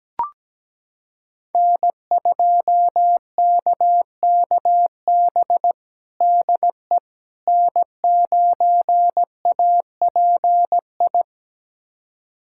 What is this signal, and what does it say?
Morse code, heard as a keyed tone, about 17 wpm, 700 Hz. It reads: N2KKB DE N9API
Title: USA Callsign DE Callsign - 5 Characters - 17wpm